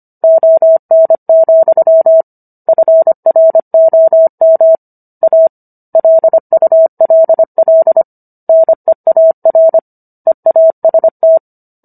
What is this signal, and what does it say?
Morse code, heard as a keyed tone, about 25 wpm, 650 Hz. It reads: ON, FROM A LULL NEAR EAST